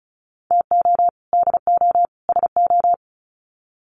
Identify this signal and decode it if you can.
Morse code, heard as a keyed tone, about 34 wpm, 700 Hz. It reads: TO DO SO
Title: Sets of 3 Words 35wpm